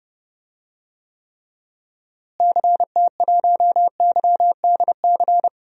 Morse code, heard as a keyed tone, about 30 wpm, 700 Hz. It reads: CT1YDC
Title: DXCC-100 Call Signs - 6 Characters - Repeated 3x - 30wpm